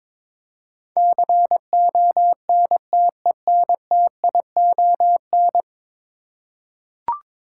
Morse code, heard as a keyed tone, about 22 wpm, 700 Hz. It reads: CONTENTION
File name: practice / 22wpm / 10 Letter Words - First 3 Letters Given - 22wpm